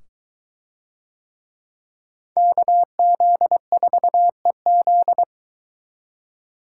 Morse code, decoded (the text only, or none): KZ4EZ